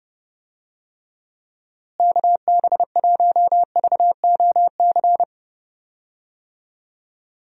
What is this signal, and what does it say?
Morse code, heard as a keyed tone, about 30 wpm, 700 Hz. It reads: KB1VOC
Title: USA Call Signs - 6 Characters - Repeated 2x - 30wpm